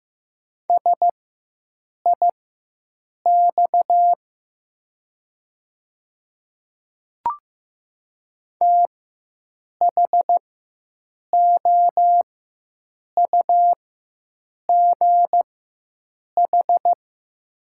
Morse code, heard as a keyed tone, about 15 wpm, spaced as for 7 wpm, 700 Hz. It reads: SIX THOUGH
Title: 4x Character Spacing - Top 500 Words - 15wpm